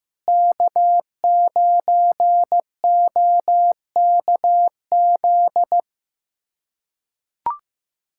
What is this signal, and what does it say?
Morse code, heard as a keyed tone, about 15 wpm, 700 Hz. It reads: K9OKZ
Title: USA Callsign DE Callsign - 5 Characters - 15wpm